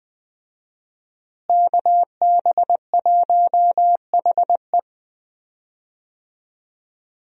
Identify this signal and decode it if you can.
Morse code, heard as a keyed tone, about 20 wpm, 700 Hz. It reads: KB1HE
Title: USA Call Signs - 5 Characters - Repeated 3x - 20wpm